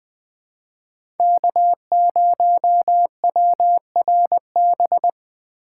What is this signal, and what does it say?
Morse code, heard as a keyed tone, about 20 wpm, 700 Hz. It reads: K0WRB
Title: USA Call Signs - 5 Characters - 4x Character Spacing - 20wpm